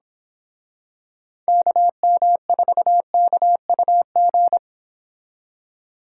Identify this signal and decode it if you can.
Morse code, heard as a keyed tone, about 26 wpm, 700 Hz. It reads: KM4KUG
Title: Speed Racing - USA 2x3 Call Signs 22wpm